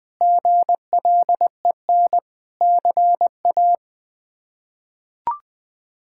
Morse code, heard as a keyed tone, about 21 wpm, 700 Hz. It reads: GLEN CA
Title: K1USN SST QSO Practice - 20wpm